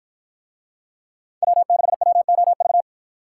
Morse code, heard as a keyed tone, about 50 wpm, 700 Hz. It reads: W6WOV